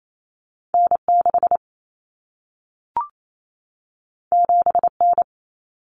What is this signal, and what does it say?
Morse code, heard as a keyed tone, about 28 wpm, 700 Hz. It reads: N6 7N
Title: Common Cut Numbers - 2-Digit 28wpm